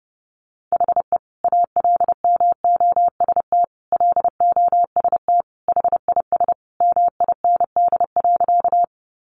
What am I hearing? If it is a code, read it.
Morse code, heard as a keyed tone, about 30 wpm, 700 Hz. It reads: HE ALMOST LOST HIS MIND.
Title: Sentences from Top 1400 Words 30wpm